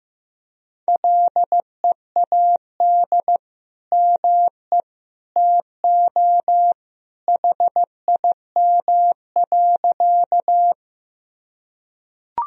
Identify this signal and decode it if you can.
Morse code, heard as a keyed tone, about 16 wpm, 700 Hz. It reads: LEAD ME TO HIM.
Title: Sentences from Top 1000 Words - Review Part 2 - 15wpm